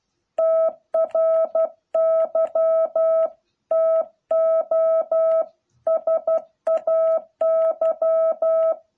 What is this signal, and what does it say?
Morse code, heard as a keyed tone, about 12 wpm, 660 Hz. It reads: TRY TO SAY